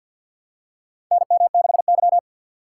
Morse code, heard as a keyed tone, about 50 wpm, 700 Hz. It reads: NM6Y
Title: USA Call Signs - 4 Characters - 3x Character Spacing - 50wpm